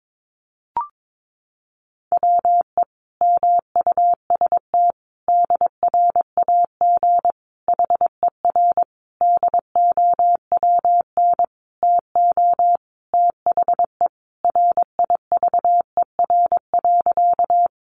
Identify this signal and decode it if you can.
Morse code, heard as a keyed tone, about 22 wpm, 700 Hz. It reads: WE MUST DRAG HER DOWN TO THE RIVER.
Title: Sentences from Top 1500 Words 22wpm